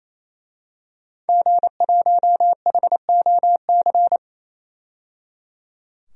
Morse code, heard as a keyed tone, about 28 wpm, 700 Hz. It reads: G1HOC
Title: DXCC-100 Call Signs - 5 Characters - 28wpm